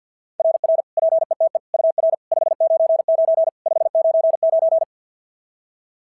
Morse code, heard as a keyed tone, about 53 wpm, 660 Hz. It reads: RR PETE UR 599 599
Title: SOTA QSO Practice 50wpm